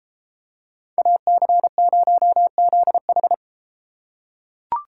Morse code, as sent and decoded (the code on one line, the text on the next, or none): .- -.-. ----- --.. ....
AC0ZH